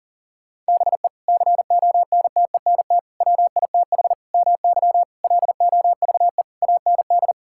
Morse code, heard as a keyed tone, about 40 wpm, 690 Hz. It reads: BE CONTENT WITH MY LOVE AND
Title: Sentences from Top 1000 Words - Review Part 6 - 40wpm